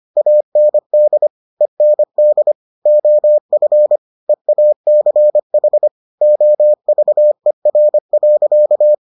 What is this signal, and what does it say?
Morse code, heard as a keyed tone, about 25 wpm, 600 Hz. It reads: AND END OF EACH OVER.